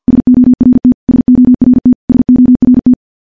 Morse code, heard as a keyed tone, about 50 wpm, 250 Hz. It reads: SORT SORT SORT
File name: morse / morse-training-50WPM